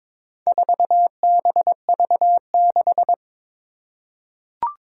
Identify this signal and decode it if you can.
Morse code, heard as a keyed tone, about 22 wpm, 700 Hz. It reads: 4BV6
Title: Four Challenging Letter-Number Combinations 22wpm